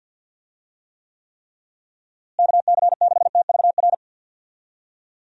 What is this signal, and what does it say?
Morse code, heard as a keyed tone, about 49 wpm, 700 Hz. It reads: KC6TVR
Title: USA Call Signs - 6 Characters - Repeated 3x - 50wpm